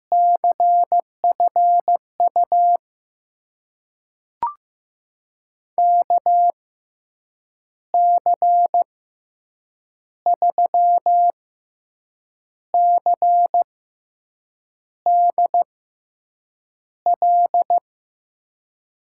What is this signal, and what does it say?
Morse code, heard as a keyed tone, about 15 wpm, spaced as for 6 wpm, 700 Hz. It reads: CFU KC3CDL
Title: USA Call Signs - 6 Characters - 6x Character Spacing - 15wpm